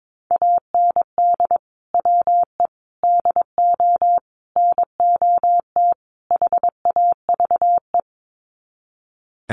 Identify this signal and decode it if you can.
Morse code, heard as a keyed tone, about 22 wpm, 700 Hz. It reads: AND WE DO NOT HAVE
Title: Sets of 5 Words Encore 22wpm